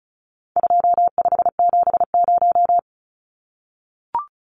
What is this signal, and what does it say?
Morse code, heard as a keyed tone, about 34 wpm, 700 Hz. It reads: 2570
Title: Numbers - 4-Digit 35wpm